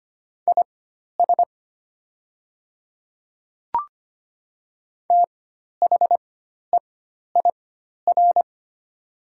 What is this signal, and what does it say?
Morse code, heard as a keyed tone, about 25 wpm, spaced as for 12 wpm, 700 Hz. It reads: IS THEIR
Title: 4x Character Spacing - Top 100 Words - 25wpm